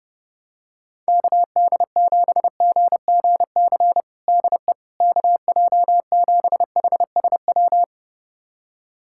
Morse code, heard as a keyed tone, about 30 wpm, 700 Hz. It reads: KD7GGC DE KJ7HSW
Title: USA Callsign DE Callsign - 6 Characters - 30wpm